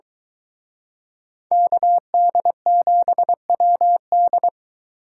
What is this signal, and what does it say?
Morse code, heard as a keyed tone, about 23 wpm, 700 Hz. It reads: KD7WD